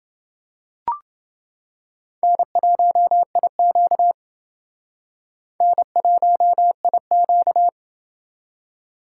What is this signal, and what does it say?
Morse code, heard as a keyed tone, about 29 wpm, 700 Hz. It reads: N1IQ N1IQ